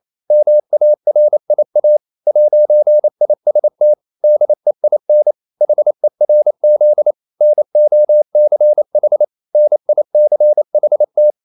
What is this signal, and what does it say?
Morse code, heard as a keyed tone, about 28 wpm, 600 Hz. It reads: MARIA 'IST DEIN HERZ NOCH NICHT
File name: marienkind9 28wpm